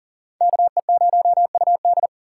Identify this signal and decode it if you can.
Morse code, heard as a keyed tone, about 52 wpm, 700 Hz. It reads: KE0UD